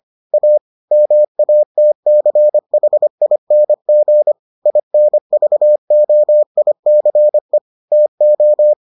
Morse code, heard as a keyed tone, about 25 wpm, 600 Hz. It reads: A MATCHING INVOICE TO